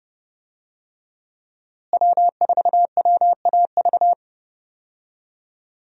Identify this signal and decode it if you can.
Morse code, heard as a keyed tone, about 30 wpm, 700 Hz. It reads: W4WAV
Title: USA Call Signs - 5 Characters - Repeated 2x - 30wpm